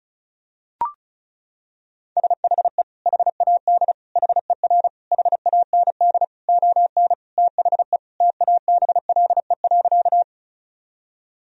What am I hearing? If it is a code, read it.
Morse code, heard as a keyed tone, about 36 wpm, 700 Hz. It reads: SHE HAD HER HAND ON THE TABLE.
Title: Sentences from Top 1000 Words - Review Part 2 - 35wpm